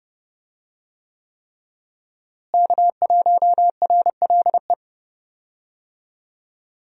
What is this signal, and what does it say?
Morse code, heard as a keyed tone, about 30 wpm, 700 Hz. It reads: K1RLE
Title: USA Call Signs - 5 Characters - Repeated 3x - 30wpm